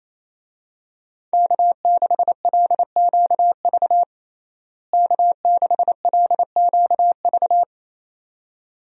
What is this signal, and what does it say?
Morse code, heard as a keyed tone, about 28 wpm, 700 Hz. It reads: K6LQV K6LQV
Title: USA Call Signs - 5 Characters - Repeated 2x - 28wpm